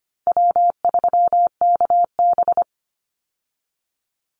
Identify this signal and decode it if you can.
Morse code, heard as a keyed tone, about 25 wpm, 700 Hz. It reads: W3KB
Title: Easy Call Signs Repeated 3x - 25wpm